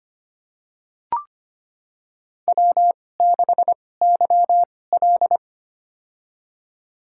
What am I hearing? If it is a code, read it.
Morse code, heard as a keyed tone, about 25 wpm, spaced as for 18 wpm, 700 Hz. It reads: W6YL
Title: USA Call Signs - 2x Character Spacing - 25wpm